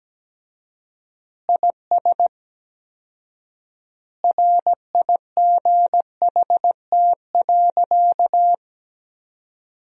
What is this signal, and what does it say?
Morse code, heard as a keyed tone, about 17 wpm, 700 Hz. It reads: IS RIGHT.
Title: 4x Word-Spacing - Sentences from Top 300 Words - 17wpm